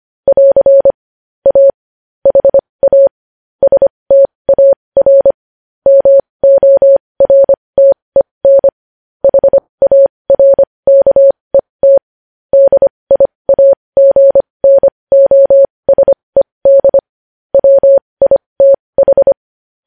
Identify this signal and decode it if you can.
Morse code, heard as a keyed tone, about 25 wpm, spaced as for 20 wpm, 550 Hz. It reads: <AR> A HA STAR MORTEN HARKET DIAGNOSED WITH
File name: podcast / News headlines-1x-25x20